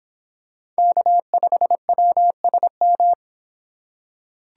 Morse code, heard as a keyed tone, about 24 wpm, 700 Hz. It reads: K5WSM